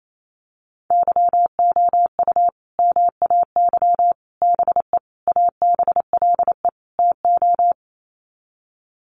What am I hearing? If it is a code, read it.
Morse code, heard as a keyed tone, about 28 wpm, 700 Hz. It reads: YOU MAY BE ABLE TO
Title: Sets of 5 Words Encore 28wpm